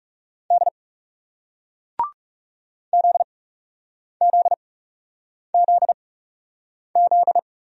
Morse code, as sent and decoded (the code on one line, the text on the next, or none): -.. / --.. --.. --.. --..
D ZZZZ